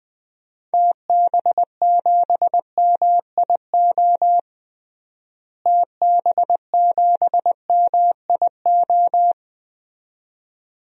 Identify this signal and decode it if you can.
Morse code, heard as a keyed tone, about 21 wpm, 700 Hz. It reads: TB7MIO TB7MIO
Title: DXCC-100 Call Signs - 6 Characters - Repeated 2x - 20wpm